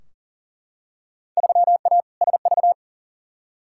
Morse code, heard as a keyed tone, about 40 wpm, 700 Hz. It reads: 3A SV
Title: Field Day Exchanges - 40wpm